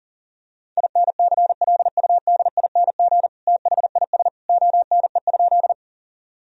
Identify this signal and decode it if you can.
Morse code, heard as a keyed tone, about 37 wpm, 700 Hz. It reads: INCLUDING THIS ONE?
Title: Sentences from Top 1000 Words - Review Part 4 - 40wpm